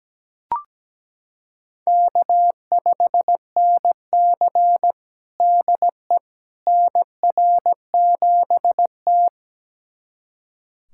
Morse code, heard as a keyed tone, about 17 wpm, 700 Hz. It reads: K5NC DE NR7T
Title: USA Callsign DE Callsign - 4 Characters - 17wpm